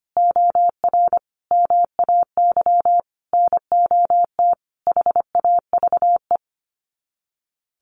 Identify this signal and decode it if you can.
Morse code, heard as a keyed tone, about 25 wpm, 700 Hz. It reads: OR MAY NOT HAVE
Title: Sets of 5 Words Encore2 25wpm